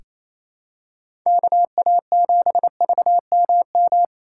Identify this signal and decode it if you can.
Morse code, heard as a keyed tone, about 28 wpm, 700 Hz. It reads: KA7VMM
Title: USA Call Signs - 6 Characters - 2x Character Spacing - 28wpm